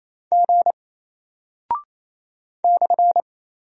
Morse code, heard as a keyed tone, about 27 wpm, 700 Hz. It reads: G /
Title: Single Character (Letter:Number:Punctuation:Prosign) 28wpm